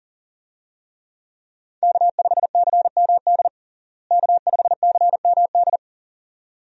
Morse code, heard as a keyed tone, about 40 wpm, 700 Hz. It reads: K5CMD K5CMD
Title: USA Call Signs - 5 Characters - Repeated 2x - 40wpm